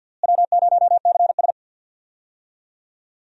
Morse code, heard as a keyed tone, about 46 wpm, 700 Hz. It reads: W0CS
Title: USA Call Signs - 4 Characters - Repeated 2x - 50wpm